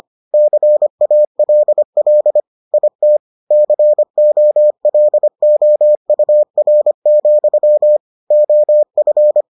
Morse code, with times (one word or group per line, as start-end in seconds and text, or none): 0.3-2.4 CALL
2.7-3.2 IT
3.5-8.0 COLOUR,
8.3-9.4 OF